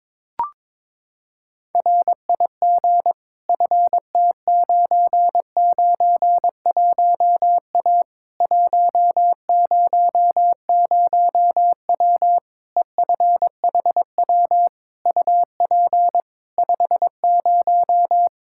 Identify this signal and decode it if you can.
Morse code, heard as a keyed tone, about 23 wpm, 700 Hz. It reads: RIG FT991A 100W EFHW UP 50
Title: Practice QSO - Exchange 2 - Info - 22wpm